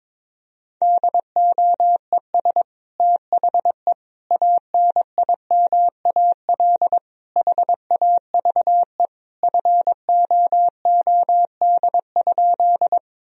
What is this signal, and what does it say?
Morse code, heard as a keyed tone, about 22 wpm, 700 Hz. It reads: DOES THE ANIMAL HAVE FOOD?
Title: Sentences from Top 1000 Words - Review Part 1 - 22wpm